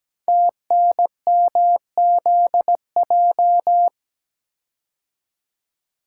Morse code, heard as a keyed tone, about 17 wpm, 700 Hz. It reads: TNMZJ
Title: Code Groups - 5 Letters - 17wpm